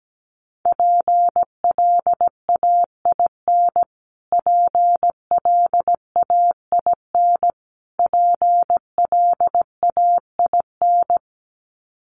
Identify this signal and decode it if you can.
Morse code, heard as a keyed tone, about 17 wpm, 700 Hz. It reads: PLAIN PLAIN PLAIN
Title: Top 500 Words Repeated 17wpm